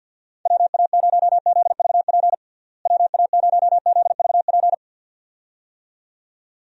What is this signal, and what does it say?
Morse code, heard as a keyed tone, about 49 wpm, 700 Hz. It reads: WA0ZVP WA0ZVP
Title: USA Call Signs - 6 Characters - Repeated 2x - 50wpm